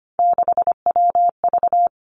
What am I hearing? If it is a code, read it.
Morse code, heard as a keyed tone, about 25 wpm, 700 Hz. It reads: 6WV
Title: Three Challenging Letter-Number Combinations 25wpm